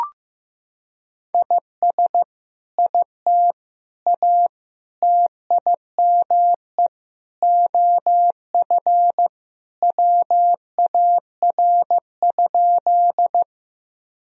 Morse code, heard as a keyed tone, about 15 wpm, 700 Hz. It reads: IS IT A TIME OF WAR?
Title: Sentences from Top 1000 Words - Review Part 2 - 15wpm